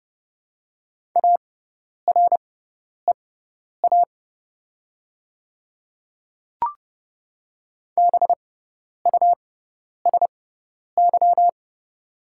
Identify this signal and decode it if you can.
Morse code, heard as a keyed tone, about 29 wpm, spaced as for 10 wpm, 700 Hz. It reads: AREA BUSY
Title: 6x Character Spacing - Top 500 Words - 30wpm